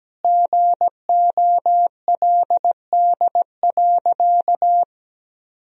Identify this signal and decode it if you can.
Morse code, heard as a keyed tone, about 17 wpm, 700 Hz. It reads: GOLD.
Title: Sentences from Top 1000 Words - Review Part 3 - 17wpm